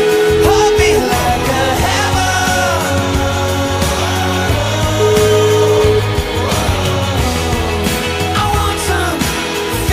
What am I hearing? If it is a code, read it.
Ringback tone (Europe, 425 Hz), with noise.